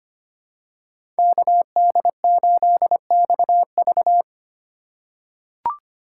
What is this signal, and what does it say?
Morse code, heard as a keyed tone, about 25 wpm, 700 Hz. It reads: KD8XV
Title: USA Call Signs - 5 Characters - 3x Character Spacing - 25wpm